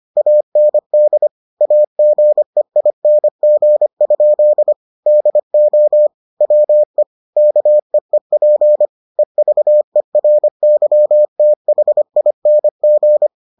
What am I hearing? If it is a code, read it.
Morse code, heard as a keyed tone, about 25 wpm, 600 Hz. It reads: AND AGEING? DO WE KEEP EVERYTHING